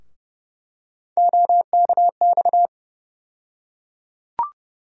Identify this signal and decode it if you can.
Morse code, heard as a keyed tone, about 31 wpm, 700 Hz. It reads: OKX
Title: USA Call Sign Postfixes - 3 Letters 30wpm